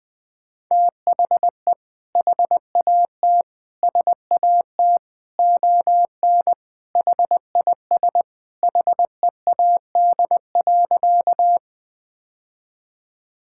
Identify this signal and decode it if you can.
Morse code, heard as a keyed tone, about 21 wpm, 700 Hz. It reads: THE HAT SAT ON HIS HEAD.
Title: Sentences with 2-Word Phrases - 20wpm